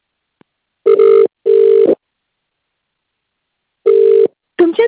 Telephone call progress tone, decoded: ringback tone